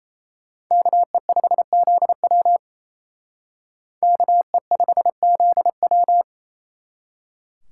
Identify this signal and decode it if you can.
Morse code, heard as a keyed tone, about 35 wpm, 700 Hz. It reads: KE5ZW KE5ZW